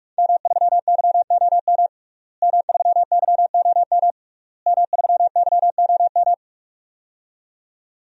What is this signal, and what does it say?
Morse code, heard as a keyed tone, about 46 wpm, 700 Hz. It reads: M3YOM M3YOM M3YOM
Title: DXCC-100 Call Signs - 5 Characters - Repeated 3x - 45wpm